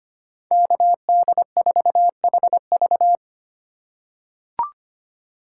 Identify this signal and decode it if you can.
Morse code, heard as a keyed tone, about 25 wpm, 700 Hz. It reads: KD4HV
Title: USA Call Signs - 5 Characters - 2x Character Spacing - 25wpm